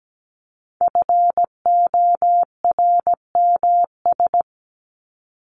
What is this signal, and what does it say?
Morse code, heard as a keyed tone, about 17 wpm, 700 Hz. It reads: FORMS
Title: ES:S Suffix - Top 100 Words with Base Words 17wpm